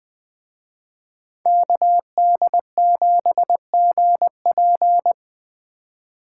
Morse code, heard as a keyed tone, about 21 wpm, 700 Hz. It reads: KD7GP
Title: Speed Racing - USA 2x2 Call Signs 17wpm